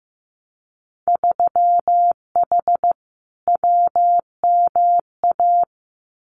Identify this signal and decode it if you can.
Morse code, heard as a keyed tone, about 15 wpm, 700 Hz. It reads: 3H WMA
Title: Winter Field Day Exchanges Encore 15wpm